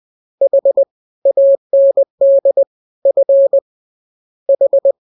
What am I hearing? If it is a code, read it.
Morse code, heard as a keyed tone, about 20 wpm, 550 Hz. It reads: H AND F H
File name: jokes / joke 19 20 0000